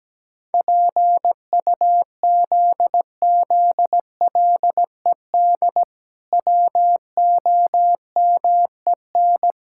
Morse code, heard as a keyed tone, about 17 wpm, 700 Hz. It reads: PUZZLED WOMEN
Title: Pangram Challenge v2 - 17wpm